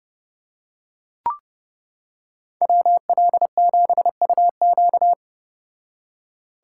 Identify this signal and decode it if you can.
Morse code, heard as a keyed tone, about 30 wpm, 700 Hz. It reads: WL7UQ